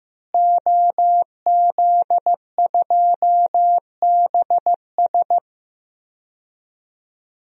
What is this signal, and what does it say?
Morse code, heard as a keyed tone, about 15 wpm, 700 Hz. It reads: OZ2BS
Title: DXCC-100 Call Signs - 5 Characters - Repeated 2x - 15wpm